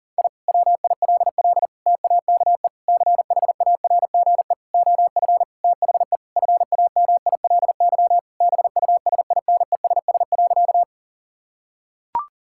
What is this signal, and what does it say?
Morse code, heard as a keyed tone, about 40 wpm, 700 Hz. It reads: I WILL TAKE CHARGE OF THE FAMILY BUSINESS.
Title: Sentences from Top 1000 Words - Review Part 6 - 40wpm